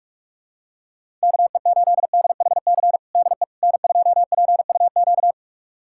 Morse code, heard as a keyed tone, about 46 wpm, 700 Hz. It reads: KE8DHC DE N2PUQ
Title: USA Callsign DE Callsign 45wpm